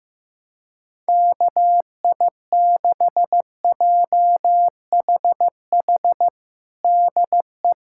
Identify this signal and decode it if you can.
Morse code, heard as a keyed tone, about 15 wpm, 700 Hz. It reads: KI6JHH DE
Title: USA Callsign DE Callsign 15wpm